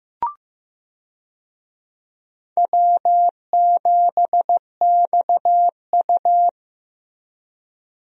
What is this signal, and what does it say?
Morse code, heard as a keyed tone, about 15 wpm, 700 Hz. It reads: W7XU
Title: USA Call Signs - 4 Characters - Repeated 2x - 15wpm